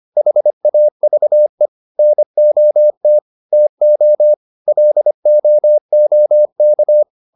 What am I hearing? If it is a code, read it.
Morse code, heard as a keyed tone, about 25 wpm, 600 Hz. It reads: HAVE NOT TO LOOK
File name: WD204a 25